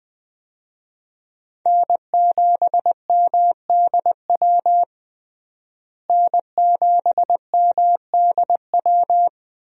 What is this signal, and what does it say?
Morse code, heard as a keyed tone, about 20 wpm, 700 Hz. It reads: N7MDW N7MDW
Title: USA Call Signs - 5 Characters - Repeated 2x - 20wpm